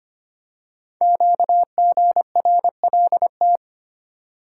Morse code, heard as a keyed tone, about 25 wpm, 700 Hz. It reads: QGRLT
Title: Code Groups - 5 Letters - 25wpm